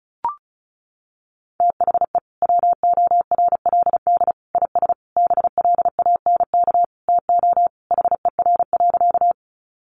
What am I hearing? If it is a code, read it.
Morse code, heard as a keyed tone, about 34 wpm, 700 Hz. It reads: THE WORLD IS BLANK TO HER.
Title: Sentences from Top 1500 Words 35wpm